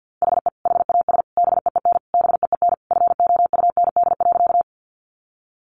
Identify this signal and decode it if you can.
Morse code, heard as a keyed tone, about 50 wpm, 700 Hz. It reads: HE HAS BEEN BEEN FOUND.
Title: Sentences from Top 300 Words 50wpm